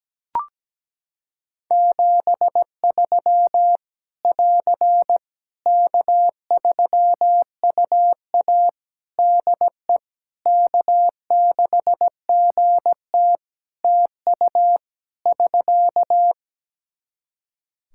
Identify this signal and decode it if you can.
Morse code, heard as a keyed tone, about 17 wpm, 700 Hz. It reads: 73 <AR> K3UA DE K6GT TU <SK>
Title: Practice QSO - Exchange 4 - Ending - 17wpm